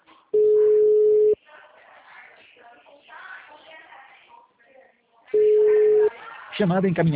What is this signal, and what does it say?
Ringback tone (Europe, 425 Hz), with speech.
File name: out- -2010- - - .53622